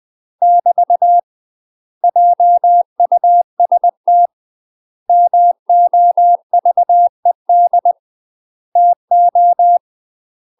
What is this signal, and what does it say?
Morse code, heard as a keyed tone, about 20 wpm, 700 Hz. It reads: <BT> JUST MOVED TO
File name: UK MASTER QSOs TRAINING / 20WPM / UK MASTER QSOs 0050